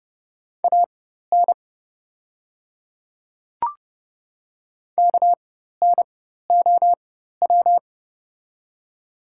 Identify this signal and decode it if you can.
Morse code, heard as a keyed tone, about 29 wpm, spaced as for 14 wpm, 700 Hz. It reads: AN KNOW
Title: 4x Character Spacing - Top 100 Words - 30wpm